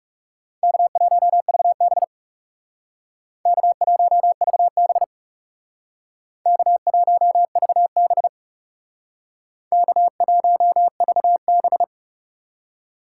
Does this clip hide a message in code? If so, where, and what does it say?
Yes: Morse code from 0.6 to 11.8 s, heard as a keyed tone, about 38 wpm, 700 Hz. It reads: K1VB K1VB K1VB K1VB